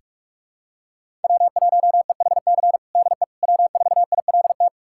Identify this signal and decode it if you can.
Morse code, heard as a keyed tone, about 46 wpm, 700 Hz. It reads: W1EHC DE W4ILT
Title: USA Callsign DE Callsign - 5 Characters - 45wpm